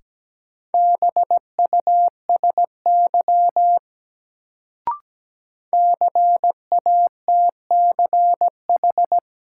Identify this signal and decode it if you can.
Morse code, heard as a keyed tone, about 17 wpm, 700 Hz. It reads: BUSY CATCH
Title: Sentences from Top 1500 Words - New Word Drill Alt - 17wpm